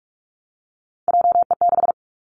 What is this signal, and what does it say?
Morse code, heard as a keyed tone, about 46 wpm, 700 Hz. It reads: JE6
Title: Three Challenging Letter-Number Combinations 45wpm